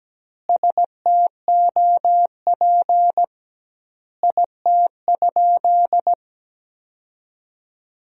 Morse code, heard as a keyed tone, about 17 wpm, 700 Hz. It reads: STOP IT?
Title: 2x Word-Spacing - Sentences from Top 300 Words - 17wpm